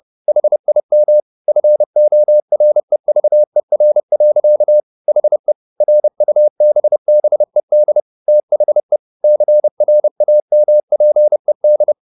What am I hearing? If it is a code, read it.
Morse code, heard as a keyed tone, about 30 wpm, 600 Hz. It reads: HIM FOREVER. HE RUBBED THE CRAMPED
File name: the old man and the sea / the old man and the sea 1 30wpm